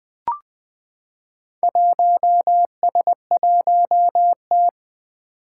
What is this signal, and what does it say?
Morse code, heard as a keyed tone, about 20 wpm, 700 Hz. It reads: 1S1T
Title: Four Challenging Letter-Number Combinations 20wpm